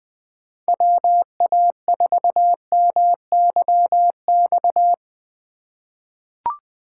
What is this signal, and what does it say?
Morse code, heard as a keyed tone, about 20 wpm, 700 Hz. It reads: WA4MYX
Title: USA Call Signs - 6 Characters - 3x Character Spacing - 20wpm